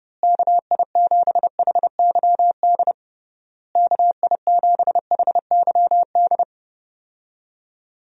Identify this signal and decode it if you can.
Morse code, heard as a keyed tone, about 30 wpm, 700 Hz. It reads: KI7HYD KI7HYD
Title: USA Call Signs - 6 Characters - Repeated 3x - 30wpm